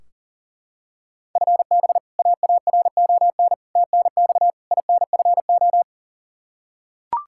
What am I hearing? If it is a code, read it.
Morse code, heard as a keyed tone, about 40 wpm, 700 Hz. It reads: FB AARON TNX INFO
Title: K1USN SST QSO Practice 40wpm